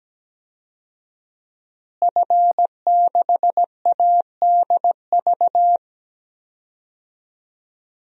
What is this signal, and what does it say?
Morse code, heard as a keyed tone, about 17 wpm, 700 Hz. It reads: F6ADV
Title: DXCC-100 Call Signs - 5 Characters - Repeated 3x - 17wpm